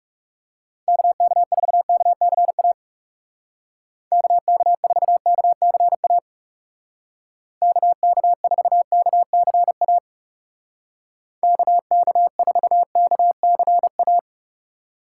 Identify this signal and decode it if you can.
Morse code, heard as a keyed tone, about 37 wpm, 700 Hz. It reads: KK4KCA KK4KCA KK4KCA KK4KCA
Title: Speed Racing - USA Call Signs 30wpm